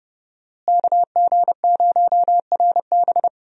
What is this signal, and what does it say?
Morse code, heard as a keyed tone, about 31 wpm, 700 Hz. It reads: KG0RB
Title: USA Call Signs - 5 Characters - Repeated 3x - 30wpm